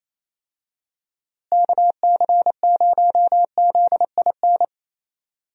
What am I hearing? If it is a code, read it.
Morse code, heard as a keyed tone, about 27 wpm, 700 Hz. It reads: KC0ZIN
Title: USA Call Signs - 6 Characters - Repeated 3x - 28wpm